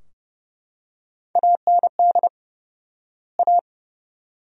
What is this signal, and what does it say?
Morse code, heard as a keyed tone, about 31 wpm, 700 Hz. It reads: AND A
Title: 4x Word-Spacing - Sentences from Top 200 Words - 30wpm